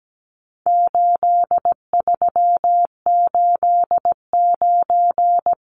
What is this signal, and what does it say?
Morse code, heard as a keyed tone, about 17 wpm, 700 Hz. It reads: 8389
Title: Numbers - 4-Digit 17wpm